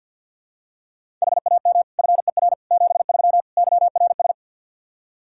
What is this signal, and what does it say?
Morse code, heard as a keyed tone, about 46 wpm, 700 Hz. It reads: HAM FER 73 YRS
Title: Practice QSO - Exchange 3 - More Info - 50wpm